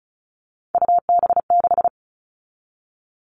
Morse code, heard as a keyed tone, about 34 wpm, 700 Hz. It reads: UB6
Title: All Cut Numbers - 3-Digit 35wpm